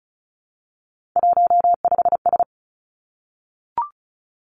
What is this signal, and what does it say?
Morse code, heard as a keyed tone, about 36 wpm, 700 Hz. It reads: 15S
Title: Three Challenging Letter-Number Combinations 35wpm